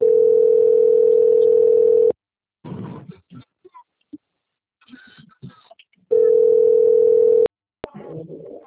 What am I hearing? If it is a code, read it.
Ringback tone (North America), 440 + 480 Hz.